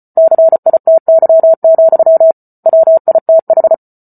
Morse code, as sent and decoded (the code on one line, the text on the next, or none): -.-. .. - -.-- --..-- / .-- .. - ....
CITY, WITH